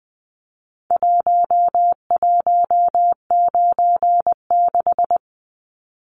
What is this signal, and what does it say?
Morse code, heard as a keyed tone, about 20 wpm, 700 Hz. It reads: 1196
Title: Numbers - Contest Style - 4-Digit 20wpm